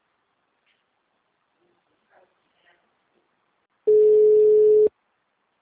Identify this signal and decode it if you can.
Ringback tone (Europe, 425 Hz).